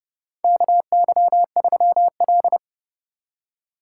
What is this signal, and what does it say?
Morse code, heard as a keyed tone, about 30 wpm, 700 Hz. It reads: KY3L
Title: USA Call Signs - 4 Characters - 3x Character Spacing - 30wpm